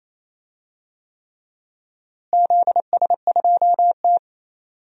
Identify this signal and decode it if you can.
Morse code, heard as a keyed tone, about 27 wpm, 700 Hz. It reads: ZS2T
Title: DXCC-100 Call Signs - 4 Characters - Repeated 3x - 28wpm